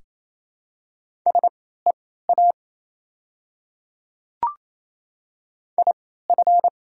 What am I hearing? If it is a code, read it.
Morse code, heard as a keyed tone, about 28 wpm, spaced as for 16 wpm, 700 Hz. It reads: SEA IF
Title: 3x Character Spacing - Top 300 Words - 28wpm